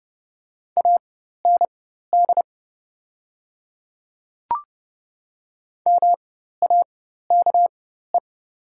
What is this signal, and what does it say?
Morse code, heard as a keyed tone, about 30 wpm, spaced as for 14 wpm, 700 Hz. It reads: AND MAKE
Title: 4x Character Spacing - Top 100 Words - 30wpm